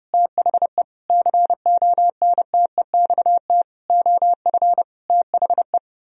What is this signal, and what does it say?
Morse code, heard as a keyed tone, about 37 wpm, 700 Hz. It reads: THE CONTEXT OF THE
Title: Speed-Racing - Sets of 5 Words 30wpm